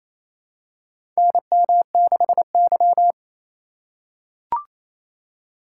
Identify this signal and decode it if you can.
Morse code, heard as a keyed tone, about 28 wpm, 700 Hz. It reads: NM6Y
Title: USA Call Signs - 4 Characters - 4x Character Spacing - 28wpm